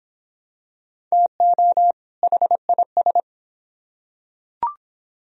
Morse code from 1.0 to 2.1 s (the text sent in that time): TO